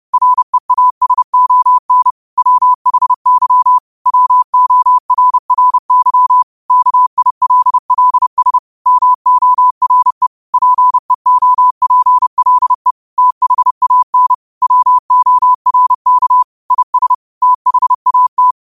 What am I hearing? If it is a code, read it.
Morse code, heard as a keyed tone, about 30 wpm, 1000 Hz. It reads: REASON WHY WORRY KILLS MORE PEOPLE THAN WORK IS THAT
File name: qotdfiles / 30wpm